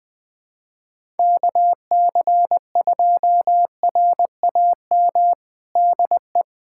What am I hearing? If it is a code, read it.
Morse code, heard as a keyed tone, about 21 wpm, 700 Hz. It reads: KC2RAM DE
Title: USA Callsign DE Callsign - 6 Characters - 20wpm